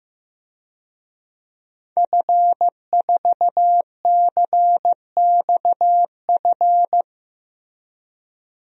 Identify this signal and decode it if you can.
Morse code, heard as a keyed tone, about 15 wpm, 700 Hz. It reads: F4CXF
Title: DXCC-100 Call Signs - 5 Characters - Repeated 3x - 15wpm